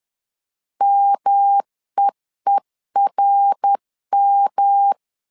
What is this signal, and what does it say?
Morse code, heard as a keyed tone, about 10 wpm, 800 Hz. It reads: MEERM
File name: LEZIONI / 3 MWRO / 3 MWR 10wpm 1952